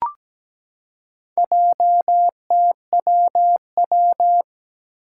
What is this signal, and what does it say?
Morse code, heard as a keyed tone, about 17 wpm, 700 Hz. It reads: JTWW
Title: Four Challenging Letter-Number Combinations 17wpm